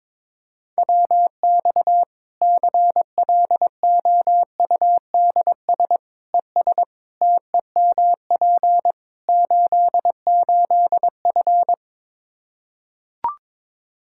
Morse code, heard as a keyed tone, about 22 wpm, 700 Hz. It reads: WX CLOUDS ES TEMP 88F
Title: K1USN SST QSO Practice 22wpm